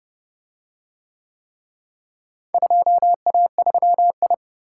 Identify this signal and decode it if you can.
Morse code, heard as a keyed tone, about 27 wpm, 700 Hz. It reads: 2A3I